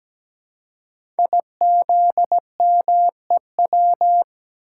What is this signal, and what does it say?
Morse code, heard as a keyed tone, about 17 wpm, 700 Hz. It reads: IZMEW
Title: Code Groups - 5 Letters - 17wpm